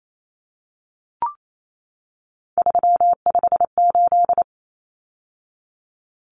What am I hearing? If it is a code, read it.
Morse code, heard as a keyed tone, about 28 wpm, 700 Hz. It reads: 358
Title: Numbers - 3-Digit 28wpm